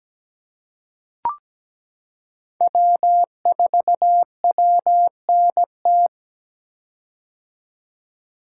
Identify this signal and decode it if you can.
Morse code, heard as a keyed tone, about 17 wpm, 700 Hz. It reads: W4WNT
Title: K1USN SST QSO Practice - 17wpm